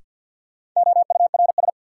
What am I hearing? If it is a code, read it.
Morse code, heard as a keyed tone, about 52 wpm, 700 Hz. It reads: OURS
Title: Sentences from Top 1000 Words - Review Part 5 - New Word Drill - 50wpm